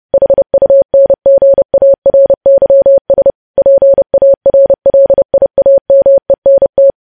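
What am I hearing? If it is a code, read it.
Morse code, heard as a keyed tone, about 30 wpm, 550 Hz. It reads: HUNGARYS PARLIAMENT
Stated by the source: News headlines sent in Morse code at 30.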